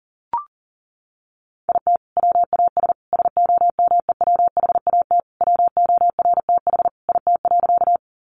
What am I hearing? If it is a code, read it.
Morse code, heard as a keyed tone, about 40 wpm, 700 Hz. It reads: IT WAS SOMEWHAT WORTH IT.
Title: Sentences from Top 1300 Words 40wpm